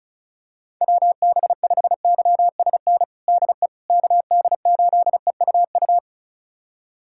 Morse code, heard as a keyed tone, about 34 wpm, 700 Hz. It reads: WB5YSN DE KD8EUU
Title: USA Callsign DE Callsign - 6 Characters - 35wpm